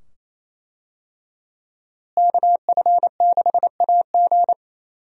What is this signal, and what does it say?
Morse code, heard as a keyed tone, about 28 wpm, 700 Hz. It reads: KF6AG
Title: USA Call Signs - 5 Characters - Repeated 2x - 28wpm